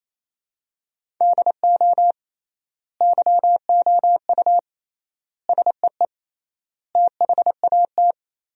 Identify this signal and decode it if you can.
Morse code, heard as a keyed tone, about 28 wpm, 700 Hz. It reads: DO YOU SEE THAT
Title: 3x Word-Spacing - Sentences from Top 500 Words - 28wpm